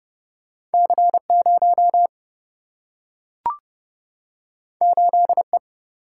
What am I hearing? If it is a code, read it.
Morse code, heard as a keyed tone, about 30 wpm, 700 Hz. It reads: C 0 8 E
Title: Single-Letter Single-Number 30wpm